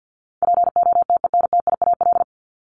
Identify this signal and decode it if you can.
Morse code, heard as a keyed tone, about 50 wpm, 700 Hz. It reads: POTENTIAL
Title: Top 501-750 Words 50wpm